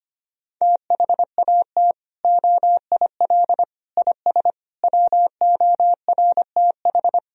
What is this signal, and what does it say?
Morse code, heard as a keyed tone, about 25 wpm, 700 Hz. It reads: THAT OIL IS WORTH
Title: Sentences from Top 1000 Words - Review Part 4 - 25wpm